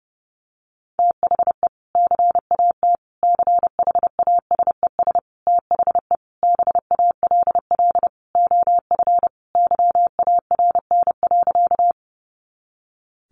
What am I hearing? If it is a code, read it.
Morse code, heard as a keyed tone, about 30 wpm, 700 Hz. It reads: THE CAT CHASES THE BALL OF YARN.
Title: Additional Easy Phrases 30wpm